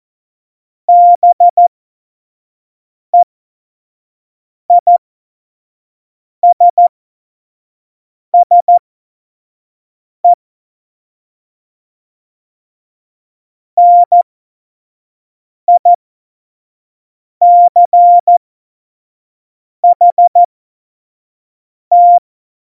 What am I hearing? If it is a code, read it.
Morse code, heard as a keyed tone, about 14 wpm, spaced as for 5 wpm, 700 Hz. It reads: BEISSE NICHT
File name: Morsekurs / Witze / Witz 26. Dezember 2017, 25 F70